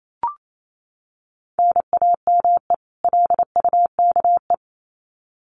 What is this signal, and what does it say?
Morse code, heard as a keyed tone, about 28 wpm, 700 Hz. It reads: NAME LUKE
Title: Common QSO Phrases 28wpm